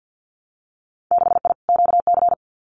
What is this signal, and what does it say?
Morse code, heard as a keyed tone, about 50 wpm, 700 Hz. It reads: 6I QC
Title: Winter Field Day Exchanges Encore 50wpm